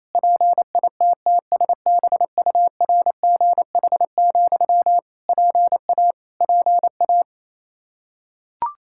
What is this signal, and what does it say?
Morse code, heard as a keyed tone, about 28 wpm, 700 Hz. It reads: PITTSBURGH, PA PA
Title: Practice QSO - Exchange 1 - Beginning - 28wpm